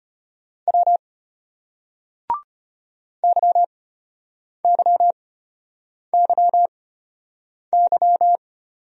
Morse code, heard as a keyed tone, about 32 wpm, spaced as for 8 wpm, 700 Hz. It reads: W YYYY